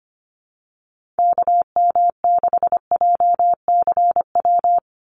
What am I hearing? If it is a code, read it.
Morse code, heard as a keyed tone, about 25 wpm, 690 Hz. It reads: KM6JCW
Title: USA Call Signs 25wpm